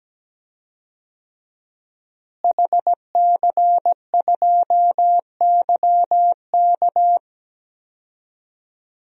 Morse code, heard as a keyed tone, about 17 wpm, 700 Hz. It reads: HC2YK
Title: DXCC-100 Call Signs - 5 Characters - Repeated 2x - 17wpm